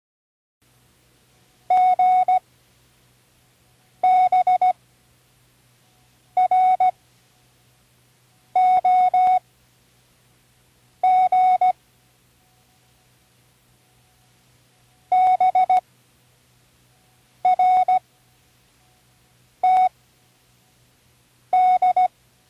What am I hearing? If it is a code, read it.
Morse code, heard as a keyed tone, about 16 wpm, spaced as for 5 wpm, 740 Hz. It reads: GBROG BRTD
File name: CWResources / Learning Morse Code (from Tune in the World) / 19 random